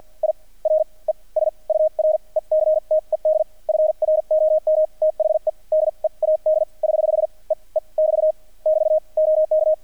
Morse code, heard as a keyed tone, about 48 wpm, spaced as for 36 wpm, 620 Hz. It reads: I RESURECTED FROM THE DEAD <HH> EE <BT> <BT> OK
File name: archived / CRSnet